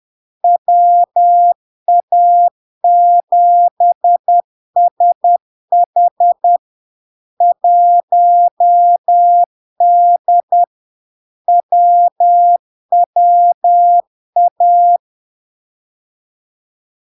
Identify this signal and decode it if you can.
Morse code, heard as a keyed tone, about 10 wpm, 700 Hz. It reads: WA7SH 1D WWA
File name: LARK-FD-2022-Log-in-CW-10wpm-700-Hz